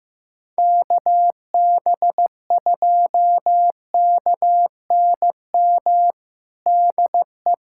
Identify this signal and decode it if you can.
Morse code, heard as a keyed tone, about 15 wpm, 700 Hz. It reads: KB2KNM DE
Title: USA Callsign DE Callsign - 6 Characters - 15wpm